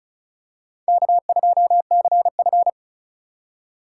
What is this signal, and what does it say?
Morse code, heard as a keyed tone, about 36 wpm, 700 Hz. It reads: K2CF
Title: USA Call Signs - 8x Character Spacing - 35wpm